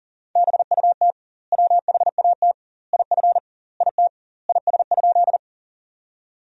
Morse code, heard as a keyed tone, about 40 wpm, 700 Hz. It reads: BUT WHAT IF IT IS?
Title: 2x Word-Spacing - Sentences from Top 100 Words - 40wpm